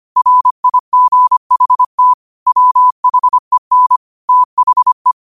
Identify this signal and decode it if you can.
Morse code, heard as a keyed tone, about 25 wpm, 1000 Hz. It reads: RIGHT WHEN THE